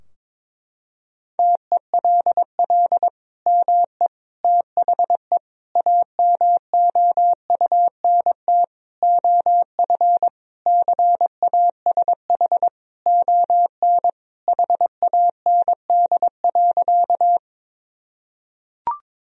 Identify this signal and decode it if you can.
Morse code, heard as a keyed tone, about 22 wpm, 700 Hz. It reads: TELL ME THE AMOUNT OF CASH ON HAND.
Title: Sentences from Top 1000 Words - Review Part 4 - 22wpm